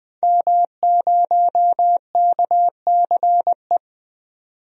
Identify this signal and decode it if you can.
Morse code, heard as a keyed tone, about 21 wpm, 700 Hz. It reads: M0KCE
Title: DXCC-100 Call Signs - 5 Characters - 20wpm